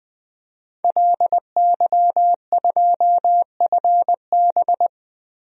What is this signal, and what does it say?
Morse code, heard as a keyed tone, about 21 wpm, 700 Hz. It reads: LY2FB
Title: DXCC-100 Call Signs - 5 Characters - Repeated 2x - 20wpm